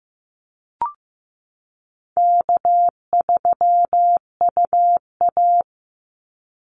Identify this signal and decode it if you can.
Morse code, heard as a keyed tone, about 15 wpm, 700 Hz. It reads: K3UA
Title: Easy Call Signs Repeated 3x - 15wpm